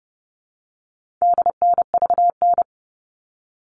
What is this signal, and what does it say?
Morse code, heard as a keyed tone, about 30 wpm, 700 Hz. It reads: DNVN
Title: All Cut Numbers - 4-Digit 30wpm